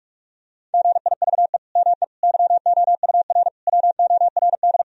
Morse code, heard as a keyed tone, about 46 wpm, 700 Hz. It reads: GIVE ME YOUR WORD
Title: Sentences from Top 1000 Words - Review Part 3 - 45wpm